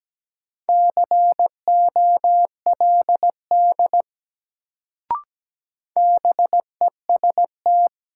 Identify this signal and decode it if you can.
Morse code, heard as a keyed tone, about 17 wpm, 700 Hz. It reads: COLD BEST
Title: Sentences from Top 400 Words - New Word Drill - 17wpm